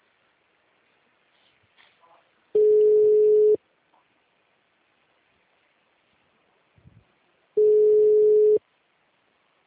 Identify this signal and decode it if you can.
Ringback tone (Europe, 425 Hz).
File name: out- -2037- - - .75631